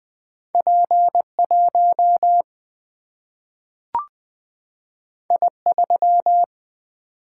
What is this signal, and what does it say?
Morse code, heard as a keyed tone, about 20 wpm, 700 Hz. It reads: P1 I3
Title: Single-Letter Single-Number 20wpm